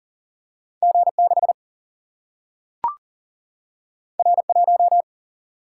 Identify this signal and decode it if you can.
Morse code, heard as a keyed tone, about 40 wpm, 700 Hz. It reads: G 6 R 1
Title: Single-Letter Single-Number 40wpm